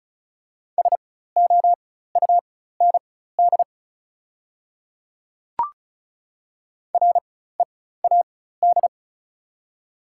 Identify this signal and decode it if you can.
Morse code, heard as a keyed tone, about 34 wpm, spaced as for 16 wpm, 690 Hz. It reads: SOUND READ
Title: 4x Character Spacing - Top 200 Words - 35wpm